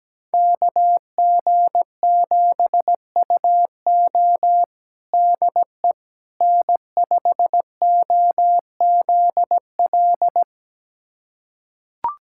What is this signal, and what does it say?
Morse code, heard as a keyed tone, about 17 wpm, 700 Hz. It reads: KG7UO DE N5OZL
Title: USA Callsign DE Callsign - 5 Characters - 17wpm